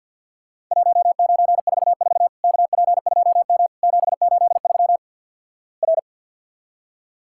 Morse code, heard as a keyed tone, about 50 wpm, 700 Hz. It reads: 1944 KP2M 783 R
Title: Worked All Europe DX Contest - QTC Exchange with Repeats - 50wpm